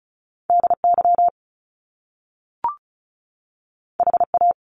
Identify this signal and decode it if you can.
Morse code, heard as a keyed tone, about 36 wpm, 700 Hz. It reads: DY HA
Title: Top Two Letter Combinations in Words 35wpm